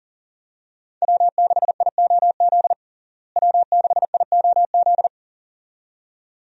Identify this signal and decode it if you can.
Morse code, heard as a keyed tone, about 40 wpm, 700 Hz. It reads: W6IOZ W6IOZ
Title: USA Call Signs - 5 Characters - Repeated 2x - 40wpm